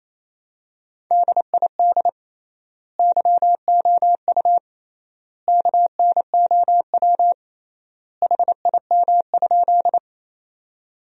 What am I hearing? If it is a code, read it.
Morse code, heard as a keyed tone, about 28 wpm, 700 Hz. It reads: DID YOU KNOW HIM?
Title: 3x Word-Spacing - Sentences from Top 100 Words - 28wpm